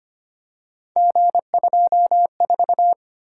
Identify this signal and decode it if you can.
Morse code, heard as a keyed tone, about 25 wpm, 700 Hz. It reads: G24
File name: Three Letter-Number Combinations - 25wpm